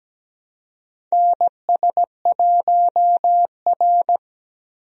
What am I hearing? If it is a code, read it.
Morse code, heard as a keyed tone, about 17 wpm, 700 Hz. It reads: NS1R
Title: USA Call Signs - 4 Characters - Repeated 3x - 17wpm